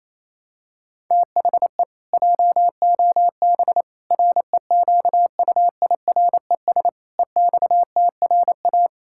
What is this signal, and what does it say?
Morse code, heard as a keyed tone, about 28 wpm, 700 Hz. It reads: THE JOB REQUIRES EXTRA
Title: Pangram Challenge v2 - 28wpm